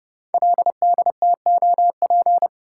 Morse code, heard as a keyed tone, about 30 wpm, 700 Hz. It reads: LDTOP
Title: Code Groups - 5 Letters - 30wpm